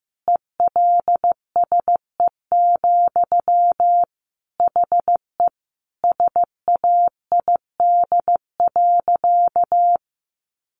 Morse code, heard as a keyed tone, about 14 wpm, 700 Hz. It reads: ELSE, HE SAID.
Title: Sentences from Top 800 Words 15wpm